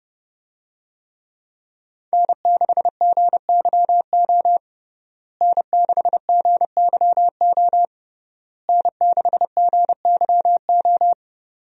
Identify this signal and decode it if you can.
Morse code, heard as a keyed tone, about 30 wpm, 700 Hz. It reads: N6GYO N6GYO N6GYO
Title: USA Call Signs - 5 Characters - Repeated 3x - 30wpm